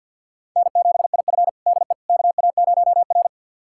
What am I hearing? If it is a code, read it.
Morse code, heard as a keyed tone, about 51 wpm, 700 Hz. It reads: N7IF DE KA0R
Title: USA Callsign DE Callsign - 4 Characters - 50wpm